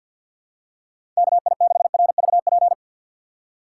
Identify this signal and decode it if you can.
Morse code, heard as a keyed tone, about 49 wpm, 700 Hz. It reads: KI6RVP
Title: USA Call Signs - 3x Character Spacing - 50wpm